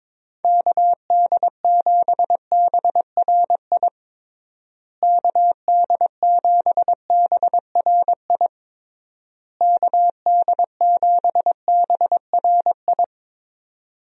Morse code, heard as a keyed tone, about 22 wpm, 700 Hz. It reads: KD7BRI KD7BRI KD7BRI
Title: USA Call Signs - 6 Characters - Repeated 3x - 22wpm